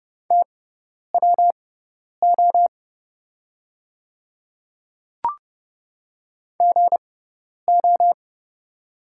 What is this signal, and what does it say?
Morse code, heard as a keyed tone, about 30 wpm, spaced as for 10 wpm, 700 Hz. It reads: TWO GO
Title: 6x Character Spacing - Top 100 Words - 30wpm